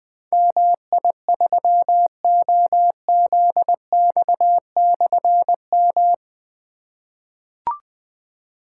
Morse code, heard as a keyed tone, about 20 wpm, 700 Hz. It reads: MI3OZX/M
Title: Difficult Call Signs 20wpm